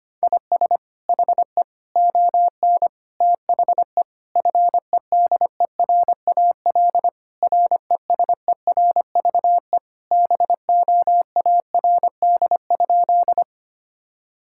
Morse code, heard as a keyed tone, about 25 wpm, 700 Hz. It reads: IS HE ON THE FEDERAL RESERVE BOARD?
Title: Sentences from Top 1000 Words - Review Part 6 - 25wpm